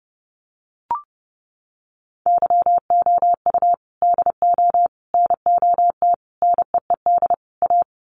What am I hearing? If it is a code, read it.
Morse code, heard as a keyed tone, about 30 wpm, 700 Hz. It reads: YOU DO NOT NEED A
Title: Sets of 5 Words Encore 30wpm